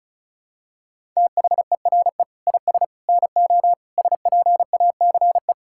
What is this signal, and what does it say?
Morse code, heard as a keyed tone, about 36 wpm, 700 Hz. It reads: THERE IS NO SPACE
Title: Sentences from Top 1000 Words - Review Part 5 - 35wpm